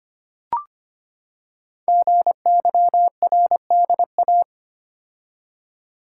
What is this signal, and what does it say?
Morse code, heard as a keyed tone, about 25 wpm, 700 Hz. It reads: GYRDA
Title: Code Groups - 5 Letters - 25wpm